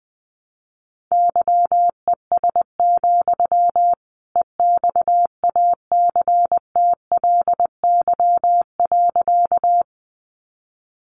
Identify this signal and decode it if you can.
Morse code, heard as a keyed tone, about 20 wpm, 700 Hz. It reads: YES, EXACTLY.
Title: Sentences from Top 900 Words 20wpm